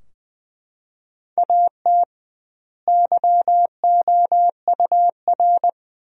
Morse code, heard as a keyed tone, about 20 wpm, 700 Hz. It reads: AT YOUR
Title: 2x Word-Spacing - Sentences from Top 200 Words - 20wpm